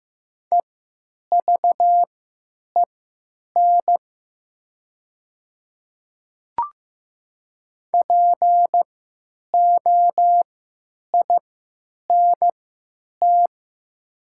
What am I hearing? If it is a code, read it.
Morse code, heard as a keyed tone, about 15 wpm, spaced as for 9 wpm, 700 Hz. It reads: EVEN POINT
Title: 3x Character Spacing - Top 300 Words - 15wpm